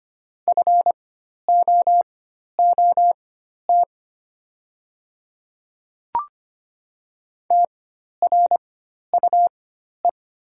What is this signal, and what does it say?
Morse code, heard as a keyed tone, about 25 wpm, spaced as for 12 wpm, 700 Hz. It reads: FOOT TRUE
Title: 4x Character Spacing - Top 500 Words - 25wpm